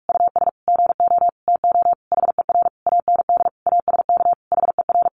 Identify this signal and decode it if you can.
Morse code, heard as a keyed tone, about 46 wpm, 700 Hz. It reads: US GO TO HER AND ASK HER
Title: Sentences from Top 1100 Words 45wpm